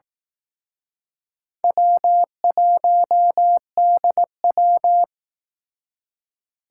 Morse code, heard as a keyed tone, about 19 wpm, 700 Hz. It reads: W1DW